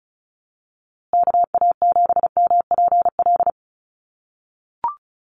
Morse code, heard as a keyed tone, about 34 wpm, 700 Hz. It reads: KA7MPL
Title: USA Call Signs Repeated 2x 35wpm